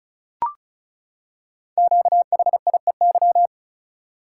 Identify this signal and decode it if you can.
Morse code, heard as a keyed tone, about 34 wpm, 690 Hz. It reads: QHIEY
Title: Code Groups - 5 Letters - 35wpm